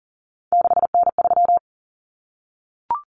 Morse code, heard as a keyed tone, about 40 wpm, 700 Hz. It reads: 6N3
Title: Common Cut Numbers - 3-Digit 40wpm